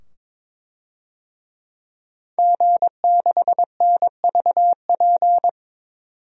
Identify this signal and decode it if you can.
Morse code, heard as a keyed tone, about 21 wpm, 700 Hz. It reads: G6NVP
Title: DXCC-100 Call Signs - 5 Characters - 22wpm